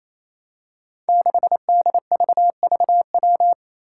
Morse code, heard as a keyed tone, about 28 wpm, 700 Hz. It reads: 6DVVW
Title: Five Challenging Letter-Number Combinations 28wpm